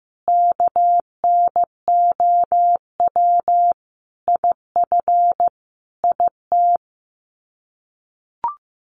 Morse code, heard as a keyed tone, about 14 wpm, 700 Hz. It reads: KNOW IF IT
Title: Sets of 5 Words 15wpm